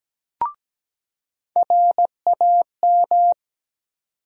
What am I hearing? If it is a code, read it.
Morse code, heard as a keyed tone, about 17 wpm, 700 Hz. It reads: RAM
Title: USA Call Sign Postfixes - 3 Letters 17wpm